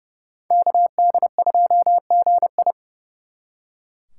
Morse code, heard as a keyed tone, about 37 wpm, 700 Hz. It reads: KD2GI
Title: Speed Racing - USA 2x2 Call Signs 30wpm